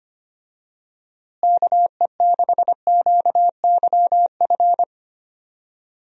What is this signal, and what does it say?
Morse code, heard as a keyed tone, about 25 wpm, 700 Hz. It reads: KE6QYF
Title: USA Call Signs - 6 Characters - Repeated 3x - 25wpm